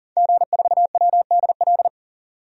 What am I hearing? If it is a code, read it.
Morse code, heard as a keyed tone, about 40 wpm, 700 Hz. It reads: G4WDL
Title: DXCC-100 Call Signs - 5 Characters - 40wpm